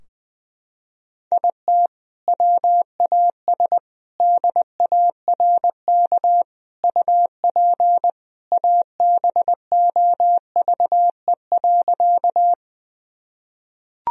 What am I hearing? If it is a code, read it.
Morse code, heard as a keyed tone, about 20 wpm, 700 Hz. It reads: IT WAS DARK UP ABOVE.
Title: Sentences from Top 1000 Words - Review Part 2 - 20wpm